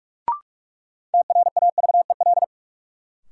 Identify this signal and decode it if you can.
Morse code, heard as a keyed tone, about 42 wpm, 700 Hz. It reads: TRAVEL
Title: Sentences from Top 400 Words - New Word Drill - 45wpm